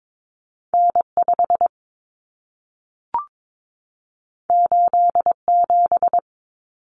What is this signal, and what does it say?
Morse code, heard as a keyed tone, about 22 wpm, 700 Hz. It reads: N5 87
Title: Common Cut Numbers - 2-Digit 22wpm